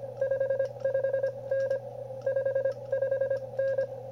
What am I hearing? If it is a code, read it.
Morse code, heard as a keyed tone, about 24 wpm, spaced as for 20 wpm, 530 Hz. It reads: 55N 55N
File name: qrp / wb0jnr at 100mW with ae5x 02-23-2018